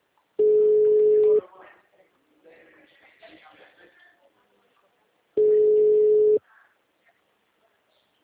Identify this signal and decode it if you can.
Ringback tone (Europe, 425 Hz), with speech.
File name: out- -2038- - - .23348